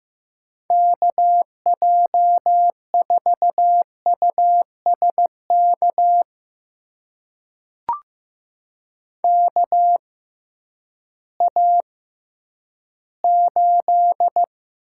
Morse code, heard as a keyed tone, about 15 wpm, spaced as for 10 wpm, 700 Hz. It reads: KJ4USK KA8
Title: USA Call Signs - 6 Characters - 6x Character Spacing - 15wpm